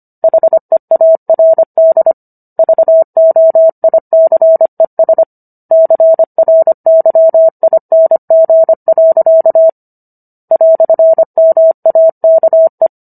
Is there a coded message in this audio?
Yes: Morse code from 0.2 to 12.9 s, heard as a keyed tone, about 25 wpm, 650 Hz. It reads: HEARD VOICES CRYING. "MAKE